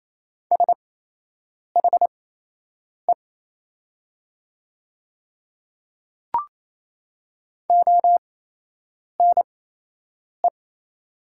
Morse code, heard as a keyed tone, about 28 wpm, spaced as for 8 wpm, 700 Hz. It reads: SHE ONE
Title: 8x Character Spacing - Top 100 Words - 28wpm